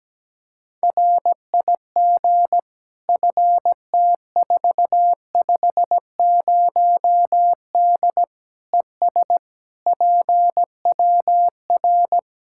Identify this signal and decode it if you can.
Morse code, heard as a keyed tone, about 17 wpm, 700 Hz. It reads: RIG FT450D ES PWR
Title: Practice QSO - Exchange 2 - Info - 17wpm